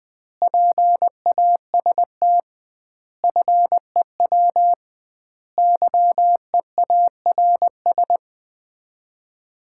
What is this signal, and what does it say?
Morse code, heard as a keyed tone, about 20 wpm, 700 Hz. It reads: PAST FEW YEARS
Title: 2x Word-Spacing - Sets of 5 Words - 20wpm